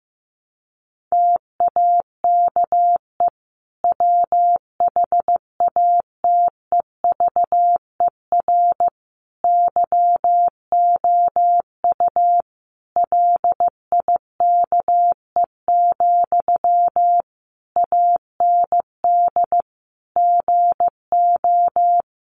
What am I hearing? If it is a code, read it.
Morse code, heard as a keyed tone, about 14 wpm, 700 Hz. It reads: TAKE WHATEVER YOU LIKE, AND GO
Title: Sentences from Top 1100 Words 15wpm